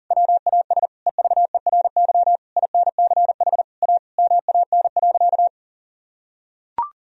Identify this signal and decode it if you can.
Morse code, heard as a keyed tone, about 37 wpm, 700 Hz. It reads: WAS EVERY INCH A MAN.
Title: Sentences from Top 1000 Words - Review Part 3 - 40wpm